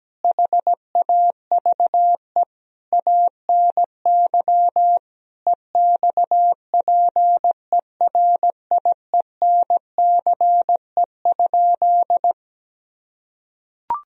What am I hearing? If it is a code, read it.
Morse code, heard as a keyed tone, about 17 wpm, 700 Hz. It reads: HAVE ANY EXPERIENCE?
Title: Sentences from Top 1000 Words - Review Part 3 - 17wpm